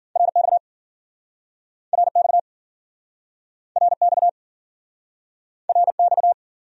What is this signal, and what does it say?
Morse code, heard as a keyed tone, about 54 wpm, 700 Hz. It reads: RX RX RX RX